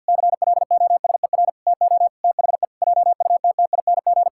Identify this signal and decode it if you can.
Morse code, heard as a keyed tone, about 50 wpm, 700 Hz. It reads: CLOSER TO THE JUTTING
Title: Pangram Challenge v2 - 50wpm